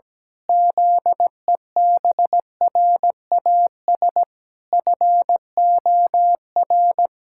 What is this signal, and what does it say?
Morse code, heard as a keyed tone, about 17 wpm, 700 Hz. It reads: ZEBRAS FOR
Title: Pangram Challenge v2 - 17wpm